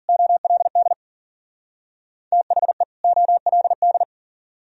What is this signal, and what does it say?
Morse code, heard as a keyed tone, about 51 wpm, 690 Hz. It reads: OLD THE OLD